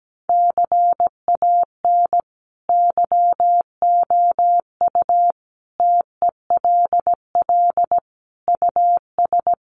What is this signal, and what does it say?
Morse code, heard as a keyed tone, about 17 wpm, 700 Hz. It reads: CAN YOU TELL US
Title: Sets of 5 Words Encore2 17wpm